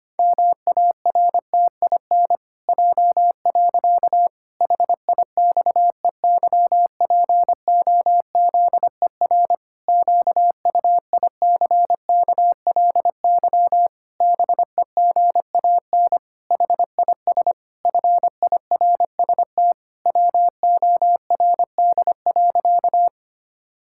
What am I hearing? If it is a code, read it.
Morse code, heard as a keyed tone, about 25 wpm, 700 Hz. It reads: MARTIN J. HIXEYPOZER QUICKLY BEGAN HIS FIRST WORD.
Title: Pangram Challenge v2 - 25wpm